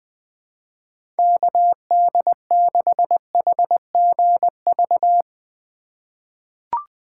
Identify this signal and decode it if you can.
Morse code, heard as a keyed tone, about 21 wpm, 700 Hz. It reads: KD6HGV
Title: USA Call Signs - 6 Characters - 2x Character Spacing - 20wpm